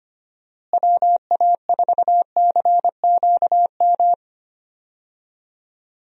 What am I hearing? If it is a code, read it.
Morse code, heard as a keyed tone, about 25 wpm, 700 Hz. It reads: WA4CQM
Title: USA Call Signs - 6 Characters - Repeated 2x - 25wpm